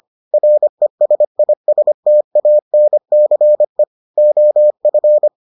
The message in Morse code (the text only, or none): RESISTANCE OF